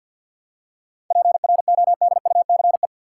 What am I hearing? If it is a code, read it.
Morse code, heard as a keyed tone, about 47 wpm, 700 Hz. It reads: PRODUCE
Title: Seven Letter Words - 50wpm